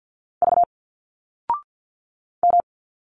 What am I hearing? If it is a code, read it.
Morse code, heard as a keyed tone, about 51 wpm, 700 Hz. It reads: V M
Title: Single Letters 50wpm